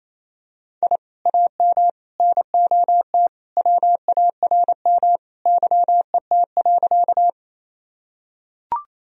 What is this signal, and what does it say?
Morse code, heard as a keyed tone, about 28 wpm, 700 Hz. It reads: I AM NOT WARM YET.
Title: Sentences from Top 1000 Words - Review Part 2 - 28wpm